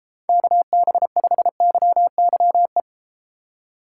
Morse code, heard as a keyed tone, about 32 wpm, 700 Hz. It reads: KB5YYE